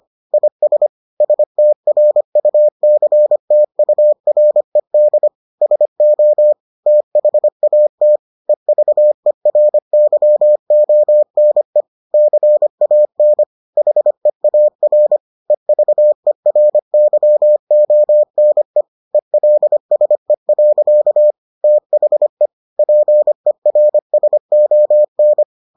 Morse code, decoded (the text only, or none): IS STRUCTURED SO THAT EVERYONE CAN HEAR EVERYONE ELSE. THE PERSON